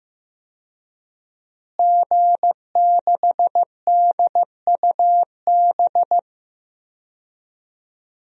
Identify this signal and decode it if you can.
Morse code, heard as a keyed tone, about 14 wpm, 700 Hz. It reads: G6DUB
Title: DXCC-100 Call Signs - 5 Characters - Repeated 2x - 15wpm